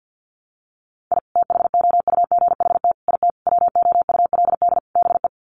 Morse code, heard as a keyed tone, about 50 wpm, 700 Hz. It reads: I THOUGHT IT WOULD BE
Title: Sets of 5 Words Encore 50wpm